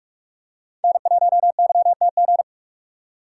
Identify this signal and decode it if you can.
Morse code, heard as a keyed tone, about 46 wpm, 700 Hz. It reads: N1YTG
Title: USA Call Signs - 5 Characters - 3x Character Spacing - 45wpm